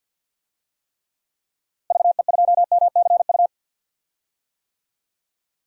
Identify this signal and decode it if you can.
Morse code, heard as a keyed tone, about 51 wpm, 700 Hz. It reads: VE2MCU
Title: DXCC-100 Call Signs - 6 Characters - Repeated 3x - 50wpm